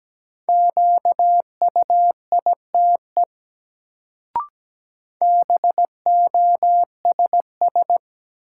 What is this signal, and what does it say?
Morse code, heard as a keyed tone, about 17 wpm, 700 Hz. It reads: QUITE BOSS
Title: Sentences from Top 700 Words - New Word Drill - 17wpm